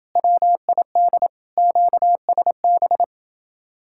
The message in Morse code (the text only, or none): WID QSB